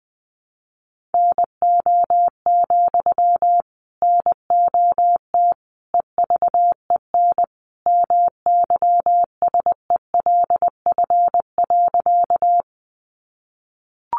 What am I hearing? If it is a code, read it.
Morse code, heard as a keyed tone, about 20 wpm, 700 Hz. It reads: NO, NOT EVEN MYSELF.
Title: Sentences from Top 1300 Words 20wpm